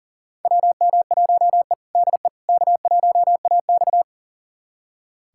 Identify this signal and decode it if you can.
Morse code, heard as a keyed tone, about 40 wpm, 700 Hz. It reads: WM1E DE K1AX
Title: USA Callsign DE Callsign - 4 Characters - 40wpm